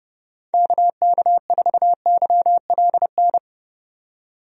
Morse code, heard as a keyed tone, about 32 wpm, 700 Hz. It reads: KK4YLN
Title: USA Call Signs - 6 Characters - 30wpm